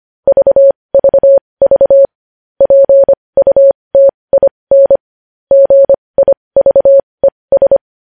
Morse code, heard as a keyed tone, about 25 wpm, spaced as for 20 wpm, 550 Hz. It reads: VVV PUTIN GIVES
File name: podcast / News headlines-1x-25x20